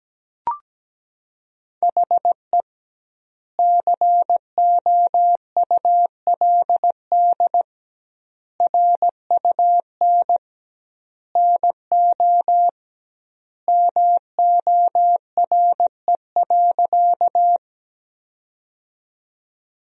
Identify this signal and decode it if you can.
Morse code, heard as a keyed tone, about 17 wpm, 700 Hz. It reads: HE COULD RUN NO MORE.
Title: 2x Word-Spacing - Sentences from Top 300 Words - 17wpm